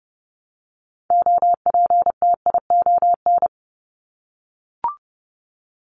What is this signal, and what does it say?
Morse code, heard as a keyed tone, about 30 wpm, 700 Hz. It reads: OPTION
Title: Top 1001-1250 Words 30wpm